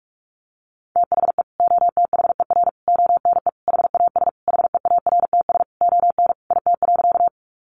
Morse code, heard as a keyed tone, about 44 wpm, 700 Hz. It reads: THE OTHER ONE HAS HEARTS ON IT.
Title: Additional Easy Phrases 45wpm